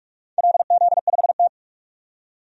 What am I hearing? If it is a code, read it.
Morse code, heard as a keyed tone, about 46 wpm, 700 Hz. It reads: LZ5T
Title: DXCC-100 Call Signs - 4 Characters - 45wpm